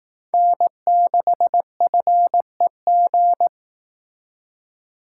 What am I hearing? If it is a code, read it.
Morse code, heard as a keyed tone, about 18 wpm, 700 Hz. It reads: N6FEG
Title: Speed Racing - USA 1x3 Call Signs 15wpm